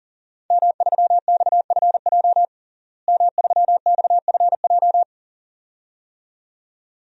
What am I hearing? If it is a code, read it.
Morse code, heard as a keyed tone, about 40 wpm, 700 Hz. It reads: M3XFJ M3XFJ
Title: DXCC-100 Call Signs - 5 Characters - Repeated 2x - 40wpm